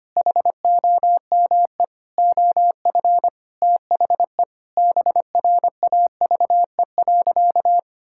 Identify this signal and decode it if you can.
Morse code, heard as a keyed tone, about 25 wpm, 700 Hz. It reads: HOME OF THE BRAVE.
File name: practice / 25wpm / Sentences with 9-Word Phrases - 25wpm